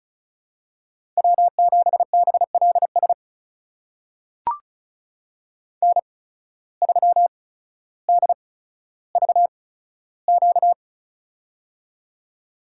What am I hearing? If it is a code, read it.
Morse code, heard as a keyed tone, about 36 wpm, spaced as for 15 wpm, 700 Hz. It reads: W7BLS N3DVQ
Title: USA Call Signs - 5 Characters - 8x Character Spacing - 35wpm